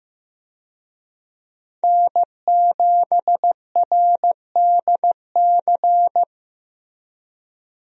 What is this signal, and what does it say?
Morse code, heard as a keyed tone, about 16 wpm, 700 Hz. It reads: N7RDC
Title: USA Call Signs - 5 Characters - Repeated 2x - 15wpm